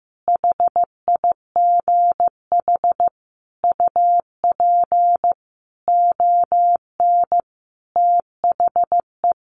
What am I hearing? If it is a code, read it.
Morse code, heard as a keyed tone, about 15 wpm, 700 Hz. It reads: HIGH UP ON THE
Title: Sentences from Top 1200 Words 15wpm